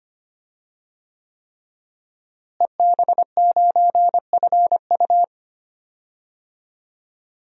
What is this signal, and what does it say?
Morse code, heard as a keyed tone, about 25 wpm, 700 Hz. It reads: EB9FU
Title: DXCC-100 Call Signs - 5 Characters - Repeated 3x - 25wpm